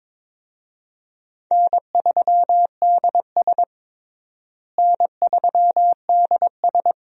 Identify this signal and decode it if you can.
Morse code, heard as a keyed tone, about 22 wpm, 700 Hz. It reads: N3DS N3DS
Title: USA Call Signs - 4 Characters - Repeated 2x - 22wpm